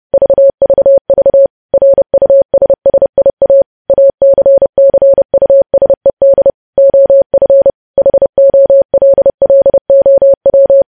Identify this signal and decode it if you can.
Morse code, heard as a keyed tone, about 30 wpm, 550 Hz. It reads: VVV RUSSIA ACCUSED OF HOLLOW
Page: news headlines in morse code at 30 wpm